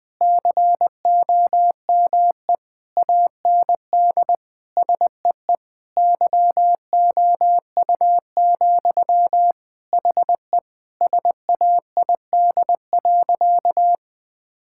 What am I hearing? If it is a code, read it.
Morse code, heard as a keyed tone, about 20 wpm, 700 Hz. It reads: COME AND SEE YOU, HE SAID.
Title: Sentences from Top 1000 Words - Review Part 6 - 20wpm